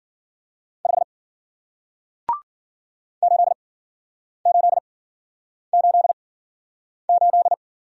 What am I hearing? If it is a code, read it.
Morse code, heard as a keyed tone, about 50 wpm, spaced as for 9 wpm, 700 Hz. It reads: 5 8888